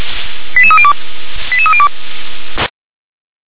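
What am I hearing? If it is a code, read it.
Five-tone selective call (ZVEI), heard as quick runs of five tones, tones of about 70 ms. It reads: 88302 88302
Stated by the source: ZVEI five-tone alert received by SDR.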